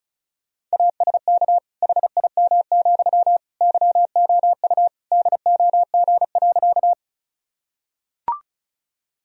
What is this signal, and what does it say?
Morse code, heard as a keyed tone, about 36 wpm, 700 Hz. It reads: ASK HIM, YOU DOG.
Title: Sentences from Top 1000 Words - Review Part 2 - 35wpm